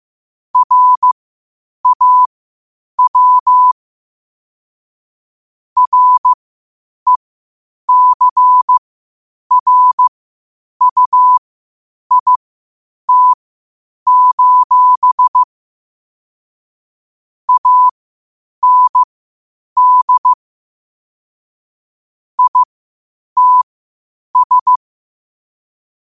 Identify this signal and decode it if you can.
Morse code, heard as a keyed tone, about 15 wpm, spaced as for 8 wpm, 1000 Hz. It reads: RAW RECRUIT: AND ITS
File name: qotdfiles / 8wpm